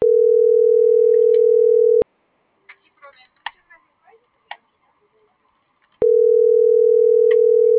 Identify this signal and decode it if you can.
Ringback tone (North America), 440 + 480 Hz.
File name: CENTRAL GENERAL / out karla.sky - .60344